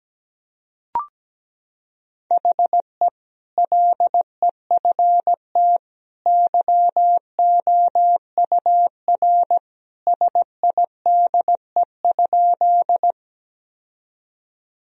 Morse code, heard as a keyed tone, about 17 wpm, 700 Hz. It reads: HE LEFT YOUR SIDE?
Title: Sentences from Top 1000 Words - Review Part 1 - 17wpm